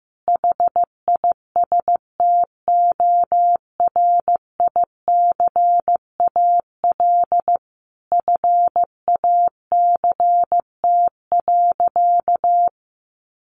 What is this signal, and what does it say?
Morse code, heard as a keyed tone, about 15 wpm, 700 Hz. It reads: HISTORICAL FACT.
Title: Sentences from Top 600 Words 15wpm